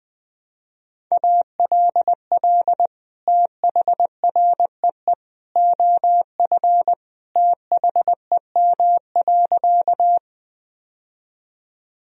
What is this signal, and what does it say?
Morse code, heard as a keyed tone, about 21 wpm, 700 Hz. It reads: ALL THREE OF THEM.
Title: Sentences from Top 1000 Words - Review Part 1 - 20wpm